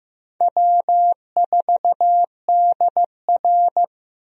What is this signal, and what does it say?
Morse code, heard as a keyed tone, about 16 wpm, 700 Hz. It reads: W4DR
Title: USA Call Signs - 4 Characters - Repeated 2x - 15wpm